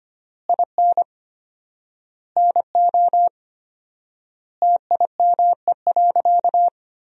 Morse code, heard as a keyed tone, about 25 wpm, 700 Hz. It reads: IN NO TIME.
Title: 4x Word-Spacing - Sentences from Top 400 Words - 25wpm